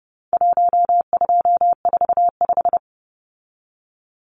Morse code, heard as a keyed tone, about 30 wpm, 700 Hz. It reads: 1245
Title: Numbers - Contest Style - 4-Digit 30wpm